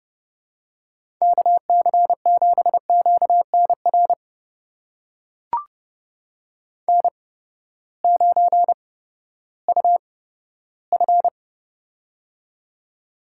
Morse code, heard as a keyed tone, about 30 wpm, spaced as for 15 wpm, 700 Hz. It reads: KC7QNR N9UF
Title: USA Call Signs - 8x Character Spacing - 30wpm